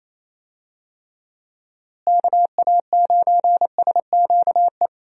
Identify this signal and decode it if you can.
Morse code, heard as a keyed tone, about 28 wpm, 700 Hz. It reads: KA9SQE
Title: USA Call Signs - 6 Characters - Repeated 2x - 28wpm